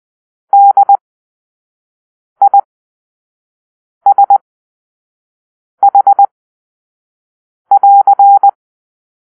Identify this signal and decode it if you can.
Morse code, heard as a keyed tone, about 20 wpm, spaced as for 5 wpm, 800 Hz. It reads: DISH<AR>
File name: arrl / Grp-2 Drill-2